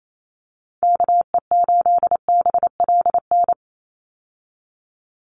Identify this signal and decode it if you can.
Morse code, heard as a keyed tone, about 28 wpm, 700 Hz. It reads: KE8BLN
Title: USA Call Signs 28wpm